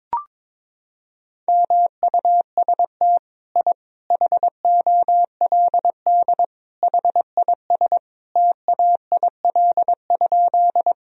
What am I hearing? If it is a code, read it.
Morse code, heard as a keyed tone, about 22 wpm, 700 Hz. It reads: MUST I HOLD HIS TAIL?
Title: Sentences from Top 1000 Words - Review Part 3 - 22wpm